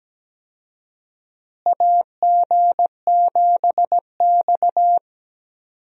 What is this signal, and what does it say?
Morse code, heard as a keyed tone, about 17 wpm, 700 Hz. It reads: AG7X
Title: USA Call Signs - 4 Characters - Repeated 3x - 17wpm